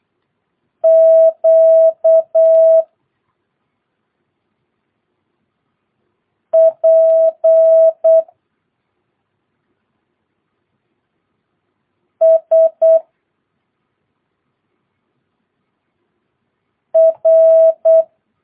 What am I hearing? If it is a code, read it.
Morse code, heard as a keyed tone, about 8 wpm, 650 Hz. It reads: Q P S R
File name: jh8cbh / 2022cw